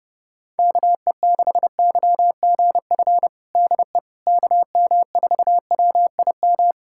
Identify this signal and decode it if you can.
Morse code, heard as a keyed tone, about 32 wpm, 700 Hz. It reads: KE6YGF DE KM4WIM
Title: USA Callsign DE Callsign - 6 Characters - 30wpm